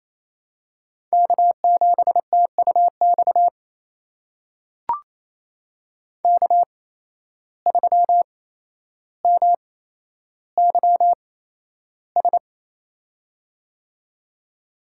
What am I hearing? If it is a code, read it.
Morse code, heard as a keyed tone, about 27 wpm, spaced as for 12 wpm, 700 Hz. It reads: K7TUX K3MYS
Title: USA Call Signs - 5 Characters - 8x Character Spacing - 28wpm